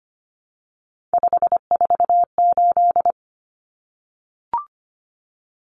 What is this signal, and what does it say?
Morse code, heard as a keyed tone, about 25 wpm, 700 Hz. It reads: 548
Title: Numbers - 3-Digit 25wpm